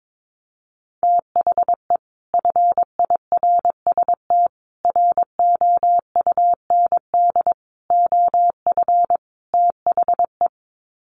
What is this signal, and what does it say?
Morse code, heard as a keyed tone, about 22 wpm, 700 Hz. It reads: THE FIRST ROUND OF THE
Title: Sets of 5 Words Encore 22wpm